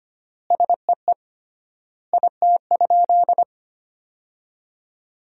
Morse code, heard as a keyed tone, about 25 wpm, 700 Hz. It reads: SEE IT?
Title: 3x Word-Spacing - Sentences from Top 100 Words - 25wpm